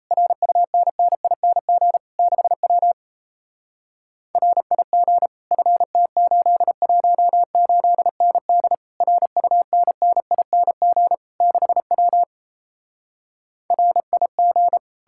Morse code, heard as a keyed tone, about 38 wpm, 700 Hz. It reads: RUNNING 6W RIG FT818ND RUNNING 6W RIG